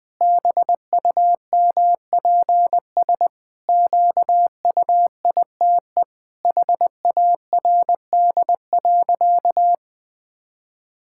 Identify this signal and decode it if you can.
Morse code, heard as a keyed tone, about 19 wpm, 700 Hz. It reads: BUMPS QUITE HARD.
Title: Pangram Challenge v2 - 20wpm